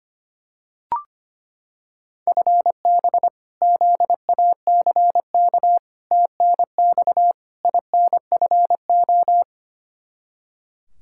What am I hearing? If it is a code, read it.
Morse code, heard as a keyed tone, about 25 wpm, 700 Hz. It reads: FB ZACK TNX INFO
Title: Practice QSO - Exchange 3 - More Info - 25wpm